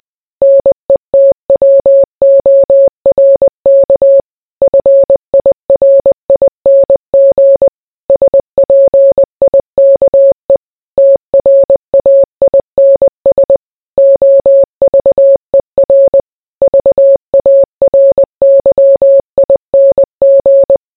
Morse code, heard as a keyed tone, about 20 wpm, 550 Hz. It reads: NETWORK FIRING SPIKE TRAINS OVER VARYING